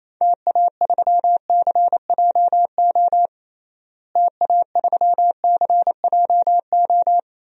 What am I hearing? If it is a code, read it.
Morse code, heard as a keyed tone, about 27 wpm, 700 Hz. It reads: TA3CJO TA3CJO
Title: DXCC-100 Call Signs - 6 Characters - Repeated 3x - 28wpm